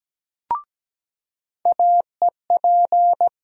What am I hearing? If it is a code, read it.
Morse code, heard as a keyed tone, about 17 wpm, 700 Hz. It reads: AEP
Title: USA Call Sign Postfixes - 3 Letters 17wpm